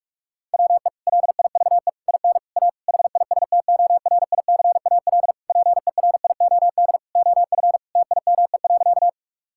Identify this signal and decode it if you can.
Morse code, heard as a keyed tone, about 46 wpm, 700 Hz. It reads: WE LIVE IN A HISTORICAL PERIOD OF TIME.
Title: Sentences from Top 1000 Words - Review Part 5 - 45wpm